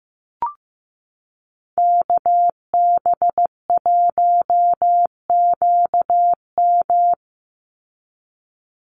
Morse code, heard as a keyed tone, about 15 wpm, 700 Hz. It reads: KB1QM
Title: USA Call Signs 15wpm